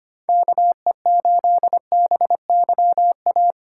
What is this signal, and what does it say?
Morse code, heard as a keyed tone, about 25 wpm, 700 Hz. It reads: KE8BYA
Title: USA Call Signs - 6 Characters - 2x Character Spacing - 25wpm